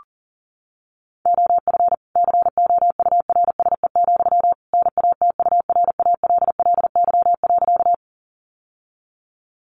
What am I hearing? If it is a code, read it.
Morse code, heard as a keyed tone, about 40 wpm, 700 Hz. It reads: OF COURSE, NATURALLY.
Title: Sentences from Top 1300 Words 40wpm